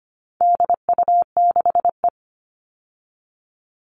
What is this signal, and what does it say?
Morse code, heard as a keyed tone, about 25 wpm, 700 Hz. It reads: DU6E
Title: All Cut Numbers - 4-Digit 25wpm